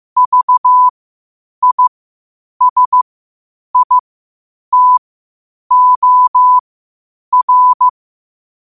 Morse code, heard as a keyed tone, about 15 wpm, spaced as for 8 wpm, 1000 Hz. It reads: VISITOR
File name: qotdfiles / 8wpm